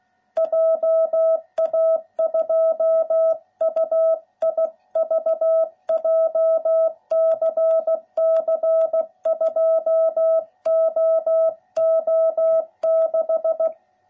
Morse code, heard as a keyed tone, about 16 wpm, 660 Hz. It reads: JA2UIV JCC 2OO6